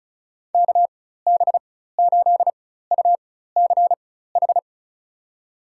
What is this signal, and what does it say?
Morse code, heard as a keyed tone, about 34 wpm, spaced as for 16 wpm, 700 Hz. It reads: KB8UCH
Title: USA Call Signs - 6 Characters - 4x Character Spacing - 35wpm